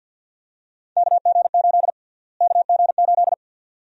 Morse code, heard as a keyed tone, about 52 wpm, 700 Hz. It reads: KG8 KG8
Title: USA Call Sign Prefixes - Repeated 2x - 50wpm